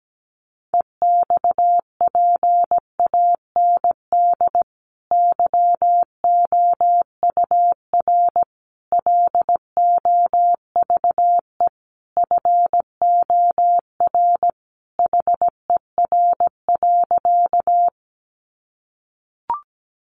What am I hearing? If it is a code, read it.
Morse code, heard as a keyed tone, about 17 wpm, 700 Hz. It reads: EXPAND YOUR LOVE FOR HER.
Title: Sentences from Top 1400 Words 17wpm